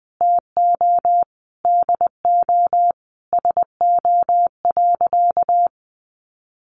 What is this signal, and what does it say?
Morse code, heard as a keyed tone, about 20 wpm, 700 Hz. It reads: TO DO SO.
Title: Sentences from Top 1500 Words 20wpm